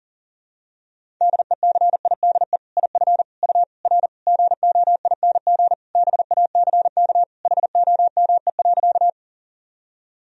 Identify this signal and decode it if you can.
Morse code, heard as a keyed tone, about 37 wpm, 700 Hz. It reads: DECIDE IF U R GOING BACK HOME.
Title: Sentences from Top 500 Words and Abbreviations 40wpm